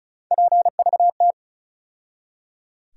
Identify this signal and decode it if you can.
Morse code, heard as a keyed tone, about 36 wpm, 700 Hz. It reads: PVT
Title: USA Call Sign Postfixes - 3 Letters 35wpm